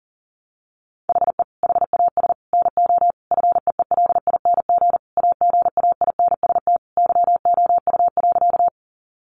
Morse code, heard as a keyed tone, about 40 wpm, 700 Hz. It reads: HE HAS NO FEELING AGAINST YOU.
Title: Sentences from Top 1200 Words 40wpm